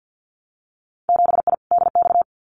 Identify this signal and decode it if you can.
Morse code, heard as a keyed tone, about 50 wpm, 700 Hz. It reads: 7I DX
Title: Winter Field Day Exchanges Encore 50wpm